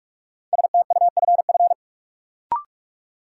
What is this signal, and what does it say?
Morse code, heard as a keyed tone, about 46 wpm, 700 Hz. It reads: STUFF
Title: Sentences from Top 1400 Words - New Word Drill Alt - 45wpm